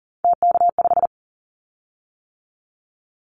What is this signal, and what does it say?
Morse code, heard as a keyed tone, about 38 wpm, 700 Hz. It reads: TK5
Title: Callsign Prefixes for Top 100 DXCC - 40wpm